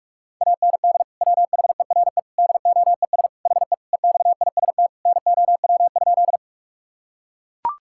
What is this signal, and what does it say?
Morse code, heard as a keyed tone, about 41 wpm, 700 Hz. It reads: AND WHERE DOES HE EXIST NOW?
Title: Sentences from Top 1000 Words - Review Part 5 - 45wpm